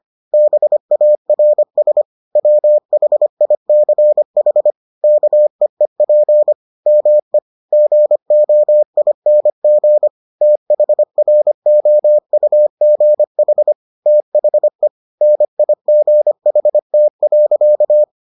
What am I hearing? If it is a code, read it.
Morse code, heard as a keyed tone, about 25 wpm, 600 Hz. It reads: BARS WHICH KEEP ME GOING THROUGH THE NIGHT.